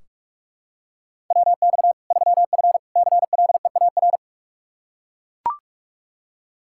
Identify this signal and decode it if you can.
Morse code, heard as a keyed tone, about 41 wpm, 700 Hz. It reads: WX 3F CLEAR
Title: K1USN SST QSO Practice 45wpm